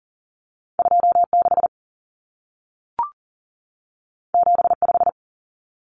Morse code, heard as a keyed tone, about 40 wpm, 700 Hz. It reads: 26 75
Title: Numbers - 2-Digit 40wpm